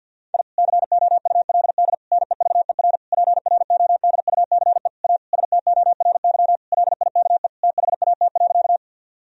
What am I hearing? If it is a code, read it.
Morse code, heard as a keyed tone, about 50 wpm, 700 Hz. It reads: I COULD NEVER PRODUCE A STORY LIKE THAT.
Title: Sentences from Top 1000 Words - Review Part 3 - 50wpm